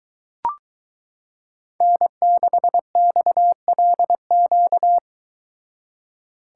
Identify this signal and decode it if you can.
Morse code, heard as a keyed tone, about 23 wpm, 700 Hz. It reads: N6XLQ